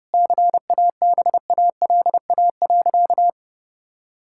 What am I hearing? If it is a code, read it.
Morse code, heard as a keyed tone, about 32 wpm, 700 Hz. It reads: CABALA.
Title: Pangram Challenge v2 - 30wpm